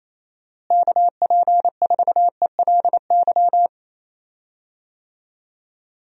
Morse code, heard as a keyed tone, about 28 wpm, 700 Hz. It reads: KP4ELY
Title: USA Call Signs - 6 Characters - Repeated 3x - 28wpm